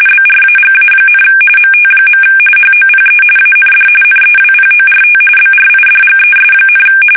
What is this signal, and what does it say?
RTTY, about 45.45 baud. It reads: BRAGA, MT. HOPE AND NEWPORT BRIDGES,
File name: fsk850 / USS-Massachusetts